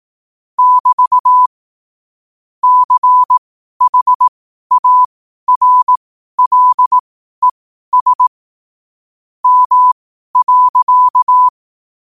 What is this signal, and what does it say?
Morse code, heard as a keyed tone, about 18 wpm, spaced as for 12 wpm, 1000 Hz. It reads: <BT> CHARLES M.
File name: qotdfiles / 12wpm